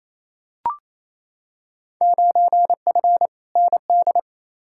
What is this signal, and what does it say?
Morse code, heard as a keyed tone, about 28 wpm, 700 Hz. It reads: 9F ND
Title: Field Day Exchanges Encore - 28wpm